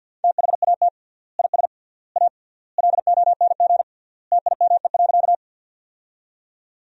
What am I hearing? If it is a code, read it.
Morse code, heard as a keyed tone, about 50 wpm, 700 Hz. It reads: THAT IS A LONG TIME.
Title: 3x Word-Spacing - Sentences from Top 100 Words - 50wpm